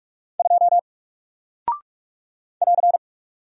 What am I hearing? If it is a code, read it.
Morse code, heard as a keyed tone, about 44 wpm, 700 Hz. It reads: 2 <AR>
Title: Single Character (Letter:Number:Punctuation:Prosign) Alt 45wpm